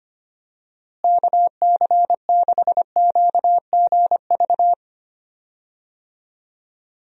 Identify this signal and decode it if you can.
Morse code, heard as a keyed tone, about 25 wpm, 700 Hz. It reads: KC6QGV
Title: USA Call Signs - 6 Characters - Repeated 3x - 25wpm